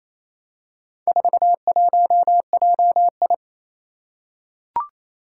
Morse code, heard as a keyed tone, about 27 wpm, 700 Hz. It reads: 41JI
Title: Four Challenging Letter-Number Combinations 28wpm